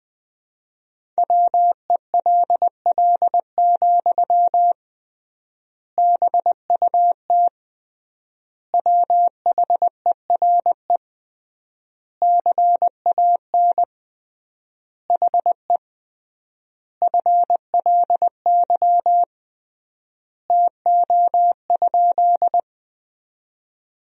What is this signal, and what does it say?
Morse code, heard as a keyed tone, about 21 wpm, 700 Hz. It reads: WELL, BUT WHERE CAN HE FLY TO?
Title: 3x Word-Spacing - Sentences from Top 500 Words - 20wpm